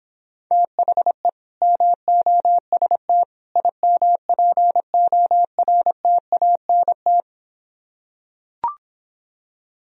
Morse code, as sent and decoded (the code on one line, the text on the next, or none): - .... / -- --- ... - / .. -- .--. --- .-. - .- -. -
THE MOST IMPORTANT